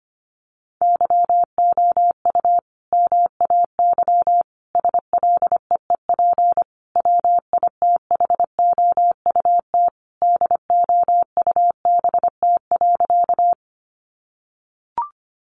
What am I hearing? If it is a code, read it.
Morse code, heard as a keyed tone, about 25 wpm, 700 Hz. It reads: YOU MAY SLEEP WITHOUT DOUBT.
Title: Sentences from Top 1500 Words 25wpm